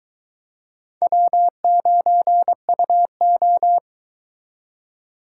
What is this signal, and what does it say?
Morse code, heard as a keyed tone, about 22 wpm, 700 Hz. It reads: W9UO